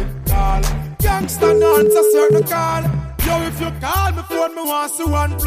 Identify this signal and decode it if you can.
Ringback tone (Europe, 425 Hz), with noise.